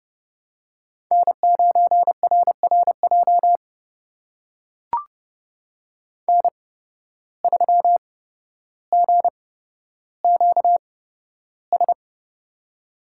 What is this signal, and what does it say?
Morse code, heard as a keyed tone, about 31 wpm, spaced as for 13 wpm, 700 Hz. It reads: N9RRJ N3GQS
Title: USA Call Signs - 5 Characters - 8x Character Spacing - 30wpm